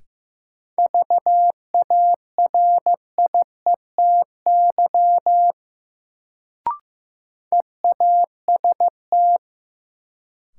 Morse code, heard as a keyed tone, about 15 wpm, 700 Hz. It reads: VARIETY EAST
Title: Sets of 3 Words - New Word Drill - 15wpm